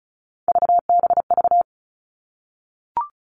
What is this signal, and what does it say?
Morse code, heard as a keyed tone, about 34 wpm, 700 Hz. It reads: VBV
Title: All Cut Numbers - 3-Digit 35wpm